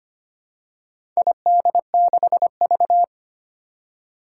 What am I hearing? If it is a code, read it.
Morse code, heard as a keyed tone, about 25 wpm, 700 Hz. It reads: ID6V
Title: Four Challenging Letter-Number Combinations 25wpm